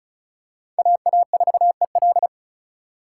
Morse code, heard as a keyed tone, about 34 wpm, 700 Hz. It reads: AA4EL
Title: USA Call Signs - 5 Characters - 4x Character Spacing - 35wpm